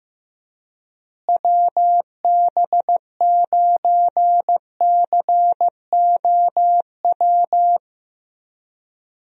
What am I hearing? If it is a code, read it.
Morse code, heard as a keyed tone, about 15 wpm, 700 Hz. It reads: WB9COW
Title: USA Call Signs - 6 Characters - Repeated 2x - 15wpm